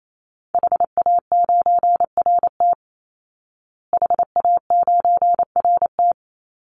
Morse code, heard as a keyed tone, about 28 wpm, 700 Hz. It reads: HA9RT HA9RT
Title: Call Signs Repeated 2x - 28wpm